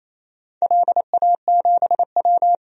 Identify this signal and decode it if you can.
Morse code, heard as a keyed tone, about 28 wpm, 700 Hz. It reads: LA7W
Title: DXCC-100 Call Signs - 4 Characters - 28wpm